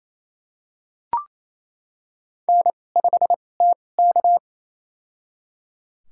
Morse code, heard as a keyed tone, about 28 wpm, spaced as for 20 wpm, 700 Hz. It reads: N5TK
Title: USA Call Signs - 4 Characters - 2x Character Spacing - 28wpm